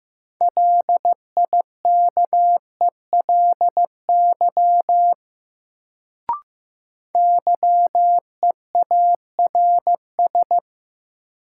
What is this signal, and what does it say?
Morse code, heard as a keyed tone, about 15 wpm, 700 Hz. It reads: LIKELY YEARS
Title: Sets of 2 Words - New Word Drill - 15wpm